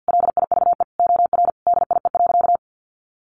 Morse code, heard as a keyed tone, about 50 wpm, 700 Hz. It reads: LIVE OR DIE.
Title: Sentences from Top 1300 Words 50wpm